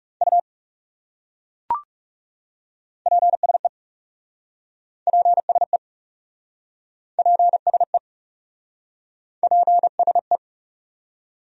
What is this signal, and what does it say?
Morse code, heard as a keyed tone, about 40 wpm, 700 Hz. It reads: U PSE PSE PSE PSE